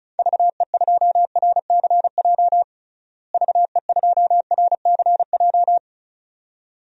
Morse code, heard as a keyed tone, about 34 wpm, 700 Hz. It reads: VE2RCJ VE2RCJ
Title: DXCC-100 Call Signs - 6 Characters - Repeated 2x - 35wpm